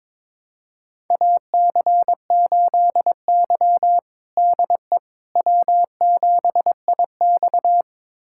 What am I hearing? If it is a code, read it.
Morse code, heard as a keyed tone, about 22 wpm, 700 Hz. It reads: AC8Y DE W7IX
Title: USA Callsign DE Callsign - 4 Characters - 22wpm